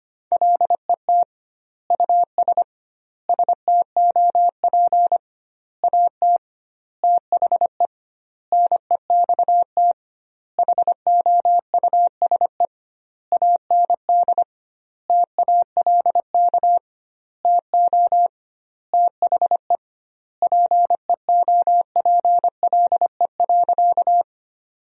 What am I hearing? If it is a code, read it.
Morse code, heard as a keyed tone, about 25 wpm, 700 Hz. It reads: LET US STOP AT THE NEXT HOUSE AND TALK TO THE PEOPLE.
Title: 2x Word-Spacing - Sentences from Top 400 Words - 25wpm